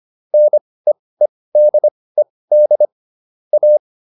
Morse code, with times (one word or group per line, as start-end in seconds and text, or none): 0.3-2.9 NEEDED
3.5-3.8 A